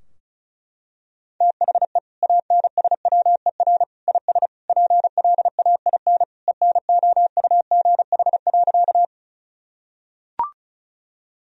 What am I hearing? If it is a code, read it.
Morse code, heard as a keyed tone, about 36 wpm, 700 Hz. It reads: THE ANSWER IS PLAIN ENOUGH.
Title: Sentences from Top 1000 Words - Review Part 2 - 35wpm